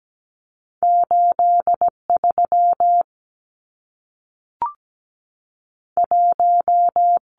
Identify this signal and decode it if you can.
Morse code, heard as a keyed tone, about 17 wpm, 700 Hz. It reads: 8 3 1
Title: Numbers - 2-Digit 17wpm